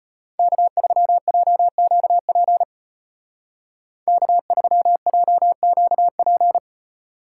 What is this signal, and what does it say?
Morse code, heard as a keyed tone, about 30 wpm, 700 Hz. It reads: K3JQP K3JQP
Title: Speed Racing - USA 1x3 Call Signs 25wpm